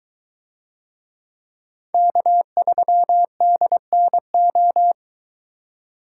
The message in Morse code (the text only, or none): K3DNO